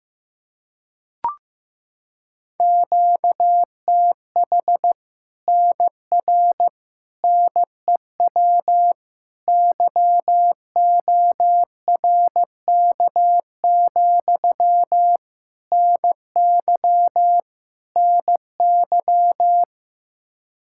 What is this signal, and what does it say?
Morse code, heard as a keyed tone, about 15 wpm, 700 Hz. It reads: QTH NR NEW YORK, NY NY
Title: Practice QSO - Exchange 1 - Beginning - 15wpm